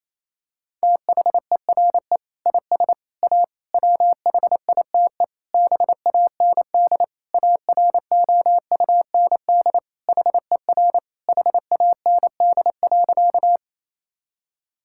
Morse code, heard as a keyed tone, about 28 wpm, 700 Hz. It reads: THERE IS A WHITE BAND AROUND HER HAND.
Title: Sentences from Top 1600 Words 28wpm